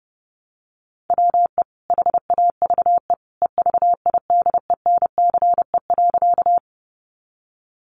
Morse code, heard as a keyed tone, about 30 wpm, 700 Hz. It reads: WE HAVE EVIDENCE.
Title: Sentences from Top 1300 Words 30wpm